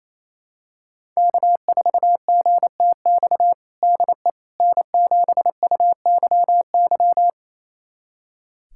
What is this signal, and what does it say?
Morse code, heard as a keyed tone, about 28 wpm, 700 Hz. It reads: K4GTX DE N7UYY
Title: USA Callsign DE Callsign 28wpm